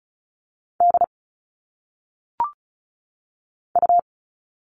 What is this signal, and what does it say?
Morse code, heard as a keyed tone, about 36 wpm, 700 Hz. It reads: D U
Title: All Cut Numbers - 1-Digit 35wpm